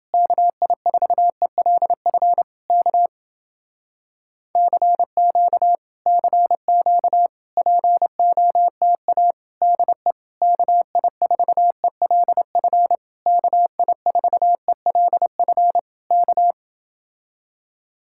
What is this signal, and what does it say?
Morse code, heard as a keyed tone, about 29 wpm, 700 Hz. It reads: KI4ELF K CQ CQ POTA DE KI4ELF KI4ELF K